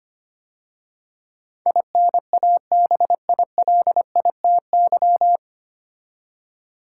Morse code, heard as a keyed tone, about 25 wpm, 700 Hz. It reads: INABILITY
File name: practice / 25wpm / 9 Letter Words - First 3 Letters Given - 25wpm